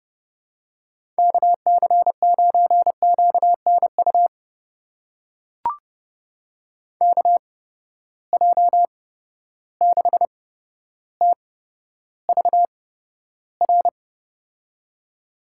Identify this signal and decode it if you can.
Morse code, heard as a keyed tone, about 28 wpm, spaced as for 13 wpm, 700 Hz. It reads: KC9QNU KJ6TVR
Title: USA Call Signs - 6 Characters - 8x Character Spacing - 30wpm